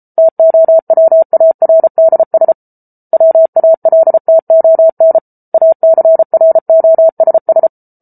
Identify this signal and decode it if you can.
Morse code, heard as a keyed tone, about 34 wpm, 650 Hz. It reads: TOWARDS WALTON ACROSS